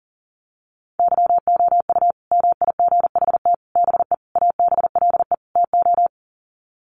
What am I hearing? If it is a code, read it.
Morse code, heard as a keyed tone, about 38 wpm, 700 Hz. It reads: YOU MIGHT BE ABLE TO
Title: Sets of 5 Words Encore2 40wpm